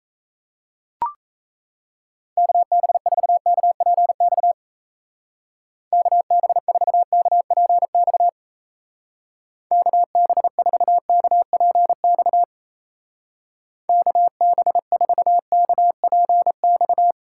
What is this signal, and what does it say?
Morse code, heard as a keyed tone, about 34 wpm, 700 Hz. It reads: KB4KPX KB4KPX KB4KPX KB4KPX
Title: Speed Racing - USA 2x3 Call Signs 28wpm